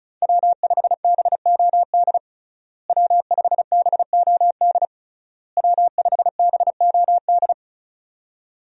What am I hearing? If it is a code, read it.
Morse code, heard as a keyed tone, about 36 wpm, 700 Hz. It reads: W5BOD W5BOD W5BOD
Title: USA Call Signs - 5 Characters - Repeated 3x - 35wpm